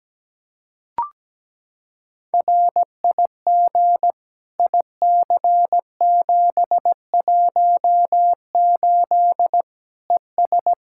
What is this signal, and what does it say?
Morse code, heard as a keyed tone, about 17 wpm, 700 Hz. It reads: RIG IC718 ES
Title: K1USN SST QSO Practice 17wpm